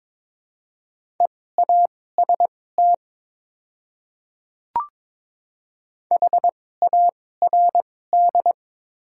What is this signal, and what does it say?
Morse code, heard as a keyed tone, about 22 wpm, spaced as for 16 wpm, 690 Hz. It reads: EAST HARD
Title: 2x Character Spacing - Top 500 Words - 22wpm